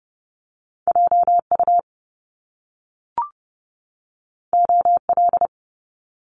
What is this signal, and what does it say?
Morse code, heard as a keyed tone, about 30 wpm, 700 Hz. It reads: JU OL
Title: Top Two Letter Combinations in Words 30wpm